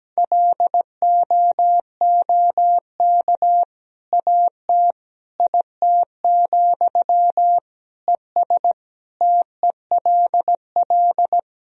Morse code, heard as a keyed tone, about 17 wpm, 700 Hz. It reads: LOOK AT IT, ES TELL
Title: Sentences from Top 200 Words and Abbreviations 17wpm